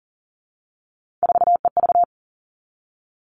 Morse code, heard as a keyed tone, about 40 wpm, 700 Hz. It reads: 4EV
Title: Three Challenging Letter-Number Combinations 40wpm